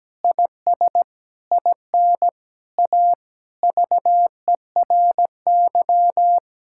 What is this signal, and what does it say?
Morse code, heard as a keyed tone, about 17 wpm, 700 Hz. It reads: IS IN A VERY
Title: Sentences from Top 1000 Words - Review Part 5 - 17wpm